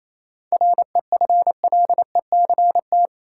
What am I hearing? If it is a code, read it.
Morse code, heard as a keyed tone, about 27 wpm, 700 Hz. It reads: REFLECT
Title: Seven Letter Words - 28wpm